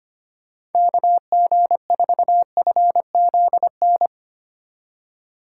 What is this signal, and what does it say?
Morse code, heard as a keyed tone, about 30 wpm, 700 Hz. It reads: KG4FZN
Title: Speed Racing - USA 2x3 Call Signs 25wpm